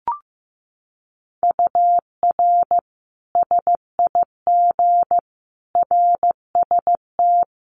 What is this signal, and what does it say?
Morse code, heard as a keyed tone, about 15 wpm, 700 Hz. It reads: UR SIG RST
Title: Common QSO Phrases 15wpm